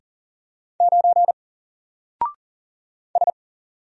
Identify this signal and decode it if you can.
Morse code, heard as a keyed tone, about 39 wpm, 700 Hz. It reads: 9 S
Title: Single Character (Letter:Number:Punctuation:Prosign) 40wpm